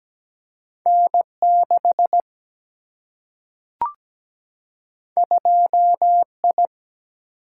Morse code, heard as a keyed tone, about 17 wpm, 700 Hz. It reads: N6 2I
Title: Single-Letter Single-Number 17wpm